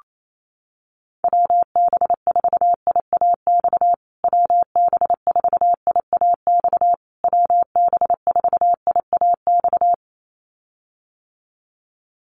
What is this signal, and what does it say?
Morse code, heard as a keyed tone, about 28 wpm, 700 Hz. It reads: WB4IAX WB4IAX WB4IAX
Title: USA Call Signs Repeated 3x 28wpm